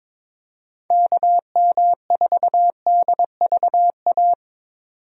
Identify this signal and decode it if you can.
Morse code, heard as a keyed tone, about 22 wpm, 700 Hz. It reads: KM4DVA
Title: USA Call Signs - 6 Characters - 22wpm